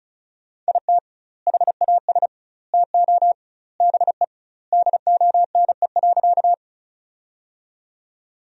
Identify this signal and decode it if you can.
Morse code, heard as a keyed tone, about 36 wpm, 700 Hz. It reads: IT HAS TO BE DONE.
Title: 2x Word-Spacing - Sentences from Top 500 Words - 35wpm